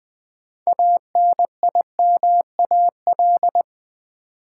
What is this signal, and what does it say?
Morse code, heard as a keyed tone, about 20 wpm, 700 Hz. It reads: ANIMAL
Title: Sentences from Top 1000 Words - Review Part 1 - New Word Drill - 20wpm